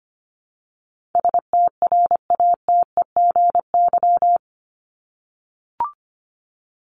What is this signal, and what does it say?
Morse code, heard as a keyed tone, about 25 wpm, 700 Hz. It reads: STRATEGY
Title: Top 501-750 Words 25wpm